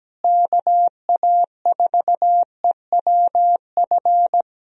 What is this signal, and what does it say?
Morse code, heard as a keyed tone, about 17 wpm, 700 Hz. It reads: KA4EWF
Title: USA Call Signs - 6 Characters - Repeated 3x - 17wpm